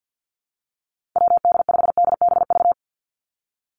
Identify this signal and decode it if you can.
Morse code, heard as a keyed tone, about 50 wpm, 700 Hz. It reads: WD5DBV
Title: Call Signs Repeated 2x - 50wpm